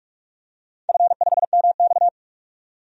Morse code, heard as a keyed tone, about 45 wpm, 700 Hz. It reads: F5MX
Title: DXCC-100 Call Signs - 4 Characters - 45wpm